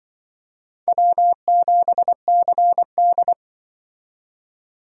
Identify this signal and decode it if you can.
Morse code, heard as a keyed tone, about 25 wpm, 700 Hz. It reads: W7CD
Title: Speed Racing - USA 1x2 Call Signs 20wpm